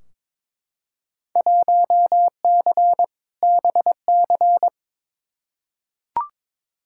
Morse code, heard as a keyed tone, about 22 wpm, 700 Hz. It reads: 1C BC
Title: Field Day Exchanges Encore - 22wpm